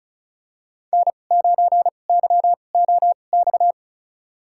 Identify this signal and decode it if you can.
Morse code, heard as a keyed tone, about 36 wpm, spaced as for 25 wpm, 700 Hz. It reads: N9YOX
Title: USA Call Signs - 5 Characters - 2x Character Spacing - 35wpm